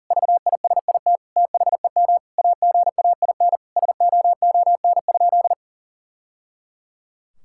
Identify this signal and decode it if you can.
Morse code, heard as a keyed tone, about 37 wpm, 690 Hz. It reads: VISIT THEM AGAIN SOON?
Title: Sentences from Top 1000 Words - Review Part 6 - 40wpm